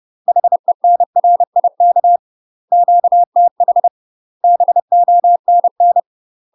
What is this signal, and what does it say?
Morse code, heard as a keyed tone, about 30 wpm, 700 Hz. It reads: HENRIK QTH BONN
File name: UK MASTER QSOs TRAINING / 30WPM / UK MASTER QSOs 0011